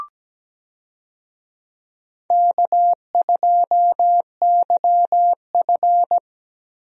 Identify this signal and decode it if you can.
Morse code, heard as a keyed tone, about 17 wpm, 700 Hz. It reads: K2YF
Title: USA Call Signs - 4 Characters - Repeated 2x - 17wpm